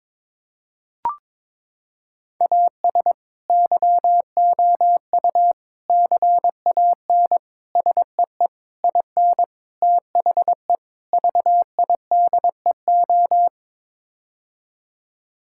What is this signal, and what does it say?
Morse code, heard as a keyed tone, about 22 wpm, 700 Hz. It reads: AS YOU CAN SEE IN THE VIDEO
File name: practice / 22wpm / Sets of 7 Words - 22wpm